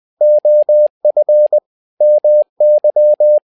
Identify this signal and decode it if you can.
Morse code, heard as a keyed tone, about 20 wpm, 600 Hz. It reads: OF MY
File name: tower trouble 20wpm